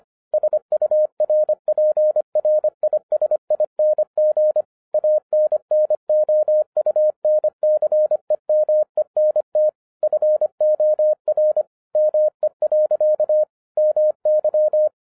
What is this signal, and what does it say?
Morse code, heard as a keyed tone, about 25 wpm, 610 Hz. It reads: SURPRISING ANNOUNCEMENT FOR ME. MY
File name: 2011-10-mp3 / n04-JUST MOVED 25wpm